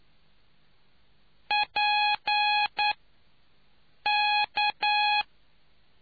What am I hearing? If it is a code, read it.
Morse code, heard as a keyed tone, about 9 wpm, spaced as for 5 wpm, 830 Hz. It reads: PK